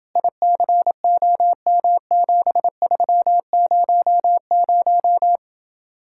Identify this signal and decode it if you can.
Morse code, heard as a keyed tone, about 27 wpm, 700 Hz. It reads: ICOM7300